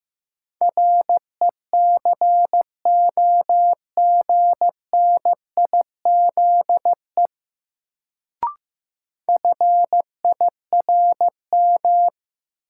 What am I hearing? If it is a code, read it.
Morse code, heard as a keyed tone, about 15 wpm, 700 Hz. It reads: RECOGNIZE FIRM
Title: Sentences from Top 1400 Words - New Word Drill Alt - 15wpm